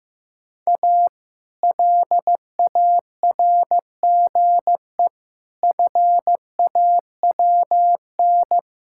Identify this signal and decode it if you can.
Morse code, heard as a keyed tone, about 15 wpm, 700 Hz. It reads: A LARGE FAWN
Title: Pangram Challenge v2 - 15wpm